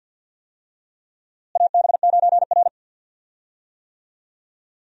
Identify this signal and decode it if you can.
Morse code, heard as a keyed tone, about 53 wpm, 700 Hz. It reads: AB9R
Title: USA Call Signs - 4 Characters - Repeated 3x - 50wpm